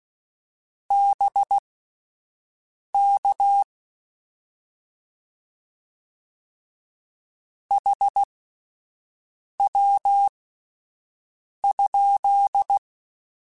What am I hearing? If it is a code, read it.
Morse code, heard as a keyed tone, about 16 wpm, spaced as for 5 wpm, 800 Hz. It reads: BK HW?
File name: Abreviations / Abreviations mp3s / Abreviations@5